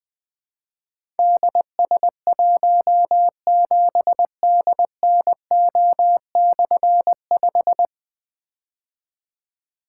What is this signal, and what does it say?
Morse code, heard as a keyed tone, about 20 wpm, 700 Hz. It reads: DS17DNO/5
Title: Difficult Call Signs 20wpm